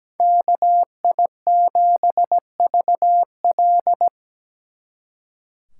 Morse code, heard as a keyed tone, about 20 wpm, 700 Hz. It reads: KI7VL
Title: Speed Racing - USA 2x2 Call Signs 17wpm